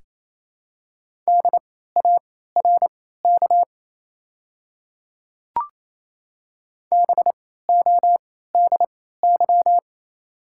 Morse code, heard as a keyed tone, about 28 wpm, spaced as for 16 wpm, 700 Hz. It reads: DARK BODY
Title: 3x Character Spacing - Top 500 Words - 28wpm